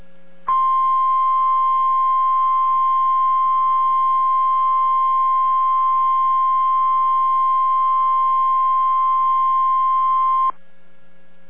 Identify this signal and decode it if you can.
Emergency Alert System (SAME) tones (the 1050 Hz attention tone). It is heard from 0.4 to 10.5 s.